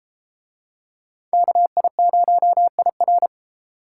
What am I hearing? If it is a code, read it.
Morse code, heard as a keyed tone, about 30 wpm, 700 Hz. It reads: KI0IR